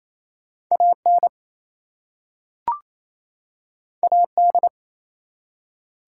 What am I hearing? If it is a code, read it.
Morse code, heard as a keyed tone, about 27 wpm, 700 Hz. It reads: AN AD
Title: Two Letter Words - 28wpm